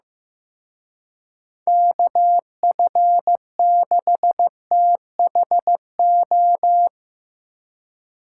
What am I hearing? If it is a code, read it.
Morse code, heard as a keyed tone, about 16 wpm, 700 Hz. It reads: KF6THO
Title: Speed Racing - USA Call Signs 15wpm